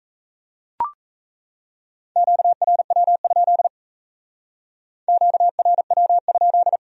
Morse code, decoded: QRW? QRW?

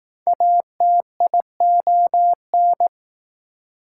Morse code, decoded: ATION

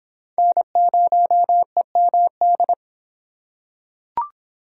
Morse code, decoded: N0EMD

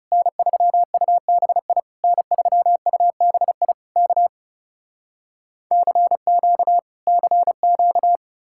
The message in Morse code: N3UBI N3UBI K CQ CQ

